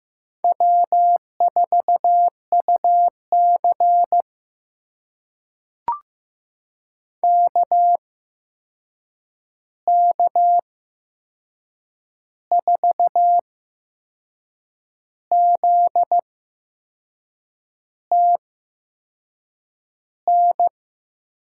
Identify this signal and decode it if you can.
Morse code, heard as a keyed tone, about 15 wpm, spaced as for 5 wpm, 700 Hz. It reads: W4UCK K4ZTN
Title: USA Call Signs - 8x Character Spacing - 15wpm